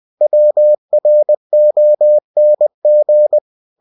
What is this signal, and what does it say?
Morse code, heard as a keyed tone, about 20 wpm, 600 Hz. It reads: WRONG
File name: cw-books / winnie-the-pooh / winnie-20wpm-page-0009